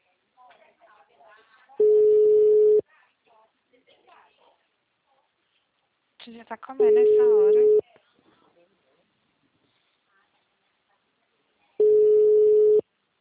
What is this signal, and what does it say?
Ringback tone (Europe, 425 Hz), with speech.
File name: out- -2051- - - .56936